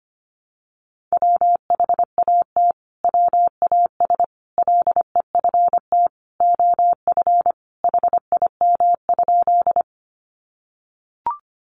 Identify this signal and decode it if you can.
Morse code, heard as a keyed tone, about 25 wpm, 700 Hz. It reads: WHAT WAS LEFT OF HIM?
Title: Sentences from Top 300 Words 25wpm